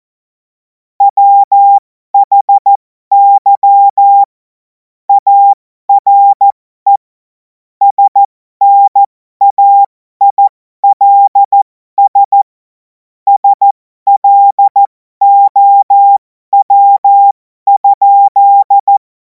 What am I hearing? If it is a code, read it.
Morse code, heard as a keyed tone, about 14 wpm, spaced as for 12 wpm, 800 Hz. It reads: WHY ARE SNAILS SLOW?
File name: Morsekurs / Witze